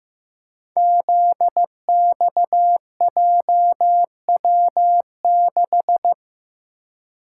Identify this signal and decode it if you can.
Morse code, heard as a keyed tone, about 15 wpm, 700 Hz. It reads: ZXJW6
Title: Code Groups - 5 Letter:Numbers - 15wpm